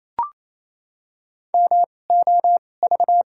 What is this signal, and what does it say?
Morse code, heard as a keyed tone, about 28 wpm, spaced as for 20 wpm, 700 Hz. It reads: MOV
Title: 2x Character Spacing - Top 300 Words - 28wpm